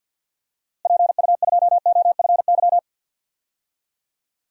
Morse code, heard as a keyed tone, about 49 wpm, 700 Hz. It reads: PU2OFY
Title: DXCC-100 Call Signs - 6 Characters - 50wpm